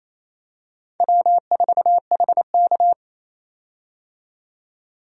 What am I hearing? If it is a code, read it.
Morse code, heard as a keyed tone, about 28 wpm, 700 Hz. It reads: W4HK
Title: USA Call Signs - 4 Characters - 28wpm